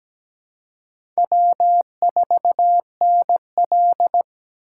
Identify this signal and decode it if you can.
Morse code, heard as a keyed tone, about 17 wpm, 700 Hz. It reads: W4NL
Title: USA Call Signs - 4 Characters - 2x Character Spacing - 17wpm